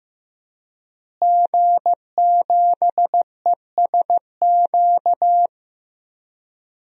Morse code, heard as a keyed tone, about 15 wpm, 700 Hz. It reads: G7ESQ
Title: DXCC-100 Call Signs - 5 Characters - Repeated 3x - 15wpm